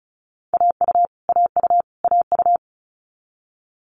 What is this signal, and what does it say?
Morse code, heard as a keyed tone, about 36 wpm, 700 Hz. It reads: AU AU AU
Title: Multi-letter Phonemes Repeated 3x 35wpm